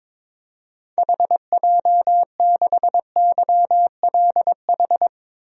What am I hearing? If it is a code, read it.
Morse code, heard as a keyed tone, about 22 wpm, 700 Hz. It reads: HJ6YLH
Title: DXCC-100 Call Signs - 6 Characters - 22wpm